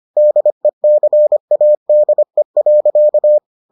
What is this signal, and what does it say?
Morse code, heard as a keyed tone, about 25 wpm, 600 Hz. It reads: DECADE.